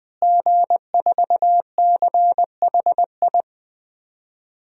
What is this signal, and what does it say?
Morse code, heard as a keyed tone, about 20 wpm, 700 Hz. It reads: G4CHI
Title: DXCC-100 Call Signs - 5 Characters - Repeated 2x - 20wpm